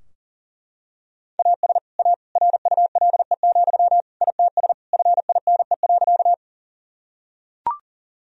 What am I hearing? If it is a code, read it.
Morse code, heard as a keyed tone, about 40 wpm, 700 Hz. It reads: AS A RULE, ITS FINE.
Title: Sentences from Top 1000 Words - Review Part 2 - 40wpm